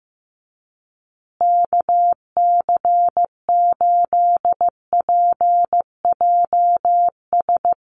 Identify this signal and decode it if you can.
Morse code, heard as a keyed tone, about 15 wpm, 700 Hz. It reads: KC8PJS
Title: USA Call Signs Repeated 3x 15wpm